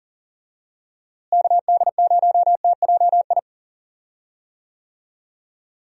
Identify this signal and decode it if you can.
Morse code, heard as a keyed tone, about 40 wpm, 700 Hz. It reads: KD0TJI
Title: USA Call Signs - 6 Characters - Repeated 3x - 40wpm